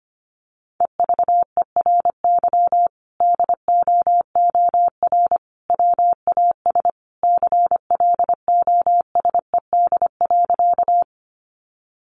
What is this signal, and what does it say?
Morse code, heard as a keyed tone, about 25 wpm, 700 Hz. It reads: EVERY DOOR WAS CLOSED.
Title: Sentences from Top 1400 Words 25wpm